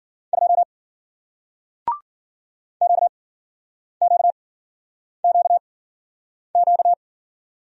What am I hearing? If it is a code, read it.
Morse code, heard as a keyed tone, about 53 wpm, spaced as for 9 wpm, 700 Hz. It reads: 2 QQQQ